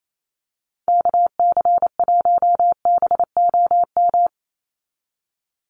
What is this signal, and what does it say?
Morse code, heard as a keyed tone, about 28 wpm, 700 Hz. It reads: KC1BOM
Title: USA Call Signs 28wpm